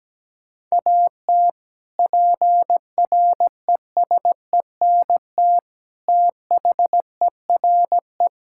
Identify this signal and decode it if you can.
Morse code, heard as a keyed tone, about 17 wpm, 700 Hz. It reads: AT PRESENT THERE
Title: Sentences from Top 1000 Words - Review Part 5 - 17wpm